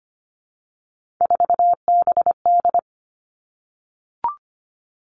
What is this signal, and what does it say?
Morse code, heard as a keyed tone, about 25 wpm, 700 Hz. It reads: 4BD
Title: Three Challenging Letter-Number Combinations 25wpm